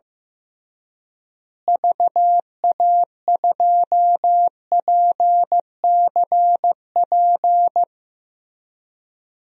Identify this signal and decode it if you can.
Morse code, heard as a keyed tone, about 15 wpm, 700 Hz. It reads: VA2PCP
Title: DXCC-100 Call Signs - 6 Characters - Repeated 3x - 15wpm